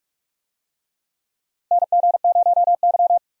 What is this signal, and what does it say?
Morse code, heard as a keyed tone, about 46 wpm, 700 Hz. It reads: NG0Y
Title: USA Call Signs - 4 Characters - Repeated 2x - 45wpm